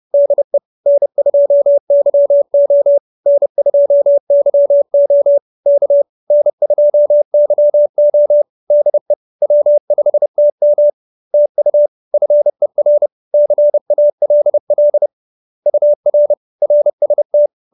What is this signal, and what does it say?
Morse code, heard as a keyed tone, about 30 wpm, 590 Hz. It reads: DE N2YO N2YO K N2YO DE W5TM TU FER CALL UR RST
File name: QSO316 30